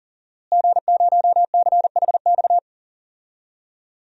Morse code, heard as a keyed tone, about 37 wpm, 690 Hz. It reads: G0CHX
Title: DXCC-100 Call Signs - 5 Characters - Repeated 2x - 40wpm